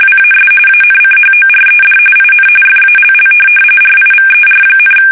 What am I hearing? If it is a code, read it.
RTTY, about 45.45 baud. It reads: WAS ANCHORED IN FALL RIVER,
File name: fsk850 / USS-Massachusetts